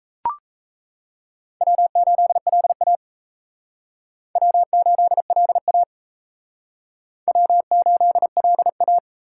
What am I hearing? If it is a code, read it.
Morse code, heard as a keyed tone, about 34 wpm, 700 Hz. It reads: W8LA W8LA W8LA
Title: Speed Racing - USA 1x2 Call Signs 28wpm